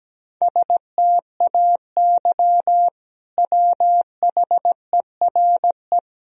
Morse code, heard as a keyed tone, about 17 wpm, 700 Hz. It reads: STAY WHERE
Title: Sentences from Top 500 Words and Abbreviations 17wpm